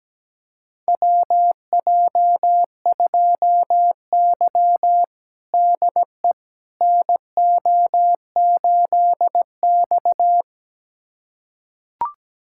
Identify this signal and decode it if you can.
Morse code, heard as a keyed tone, about 17 wpm, 700 Hz. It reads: WJ2Y DE NO8X
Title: USA Callsign DE Callsign - 4 Characters - 17wpm